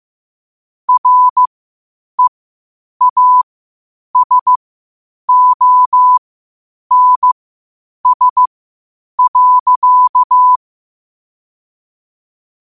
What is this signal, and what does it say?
Morse code, heard as a keyed tone, about 15 wpm, spaced as for 8 wpm, 1000 Hz. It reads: REASONS.
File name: qotdfiles / 8wpm